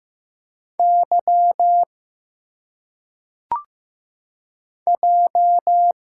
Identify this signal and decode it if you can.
Morse code, heard as a keyed tone, about 15 wpm, 700 Hz. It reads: Y J
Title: Single Letters - Phonetic Pronunciation 15wpm